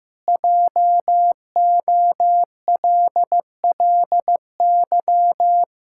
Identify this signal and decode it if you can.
Morse code, heard as a keyed tone, about 16 wpm, 700 Hz. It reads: JOLLY
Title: Pangram Challenge v2 - 15wpm